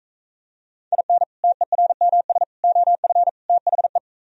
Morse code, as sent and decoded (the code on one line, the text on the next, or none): .. -. / - . .-. -- ... / --- ..-. / - .... .
IN TERMS OF THE